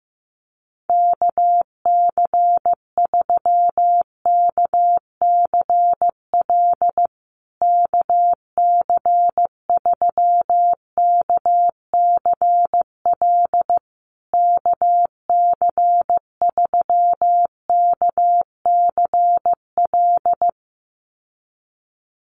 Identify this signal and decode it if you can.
Morse code, heard as a keyed tone, about 15 wpm, 700 Hz. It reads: KC3KCL KC3KCL KC3KCL
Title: USA Call Signs Repeated 3x 15wpm